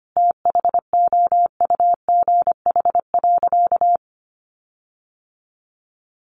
Morse code, heard as a keyed tone, about 25 wpm, 700 Hz. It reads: THOUGH.
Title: Sentences from Top 1500 Words 25wpm